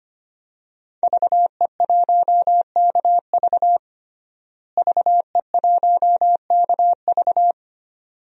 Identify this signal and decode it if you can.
Morse code, heard as a keyed tone, about 25 wpm, 700 Hz. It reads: VE1KV VE1KV
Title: DXCC-100 Call Signs - 5 Characters - Repeated 2x - 25wpm